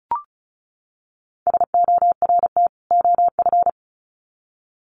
Morse code, heard as a keyed tone, about 34 wpm, 700 Hz. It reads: SORT OF
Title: Sets of 2 Words 35wpm